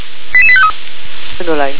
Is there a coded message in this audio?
Yes: five-tone selective call (ZVEI) from 0.3 to 0.7 s, heard as quick runs of five tones, tones of about 70 ms, with speech. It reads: 88853